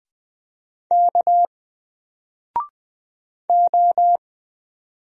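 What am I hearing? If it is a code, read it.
Morse code, heard as a keyed tone, about 20 wpm, 700 Hz. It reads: K O
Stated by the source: Single letter and number Morse drill at 20 wpm, from N4LCD.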